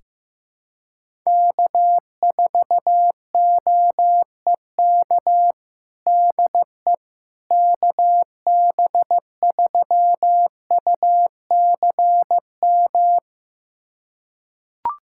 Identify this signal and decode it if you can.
Morse code, heard as a keyed tone, about 15 wpm, 700 Hz. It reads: K4OEK DE KB3UCM
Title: USA Callsign DE Callsign 15wpm